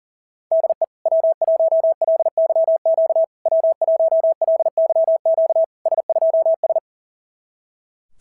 Morse code, heard as a keyed tone, about 40 wpm, 650 Hz. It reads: DE W1LYQ W1LYQ S2S
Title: SOTA QSO Practice 40wpm